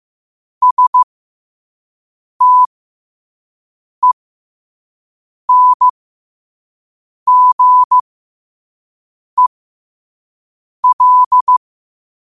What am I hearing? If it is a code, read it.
Morse code, heard as a keyed tone, about 15 wpm, spaced as for 5 wpm, 1000 Hz. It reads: STENGEL
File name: qotdfiles / 5wpm